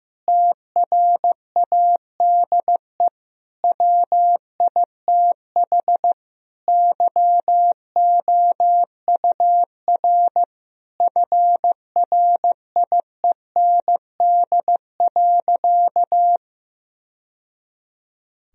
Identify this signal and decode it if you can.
Morse code, heard as a keyed tone, about 16 wpm, 700 Hz. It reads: TRADE WITH YOUR FRIEND.
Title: Sentences from Top 1000 Words - Review Part 4 - 15wpm